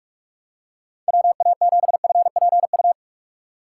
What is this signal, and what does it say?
Morse code, heard as a keyed tone, about 43 wpm, 700 Hz. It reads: WA7FPU
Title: USA Call Signs - 6 Characters - 2x Character Spacing - 45wpm